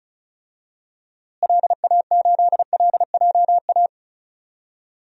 Morse code, heard as a keyed tone, about 36 wpm, 700 Hz. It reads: LA8LJA
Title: DXCC-100 Call Signs - 6 Characters - Repeated 2x - 35wpm